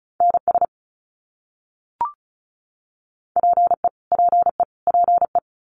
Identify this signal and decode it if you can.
Morse code, heard as a keyed tone, about 34 wpm, 700 Hz. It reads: NS PE PE PE
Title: Canadian Provinces and Territories - Repeated 3x - 35wpm